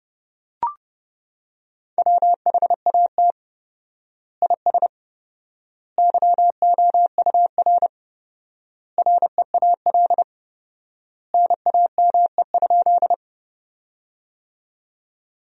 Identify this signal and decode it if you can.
Morse code, heard as a keyed tone, about 30 wpm, 700 Hz. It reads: WHAT IS YOUR REAL NAME?
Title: 4x Word-Spacing - Sentences from Top 300 Words - 30wpm